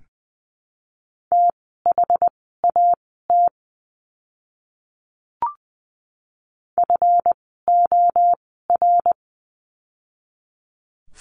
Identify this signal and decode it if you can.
Morse code, heard as a keyed tone, about 20 wpm, spaced as for 14 wpm, 700 Hz. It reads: THAT FOR
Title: 244 Lesson 39f - Letter z - Words 20wpm